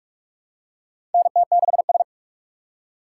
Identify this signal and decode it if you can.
Morse code, heard as a keyed tone, about 44 wpm, 700 Hz. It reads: NT6S
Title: USA Call Signs - 4 Characters - 8x Character Spacing - 45wpm